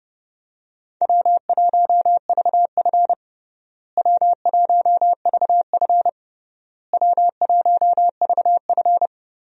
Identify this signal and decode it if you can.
Morse code, heard as a keyed tone, about 30 wpm, 700 Hz. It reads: W1VF W1VF W1VF
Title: USA Call Signs - 4 Characters - Repeated 3x - 30wpm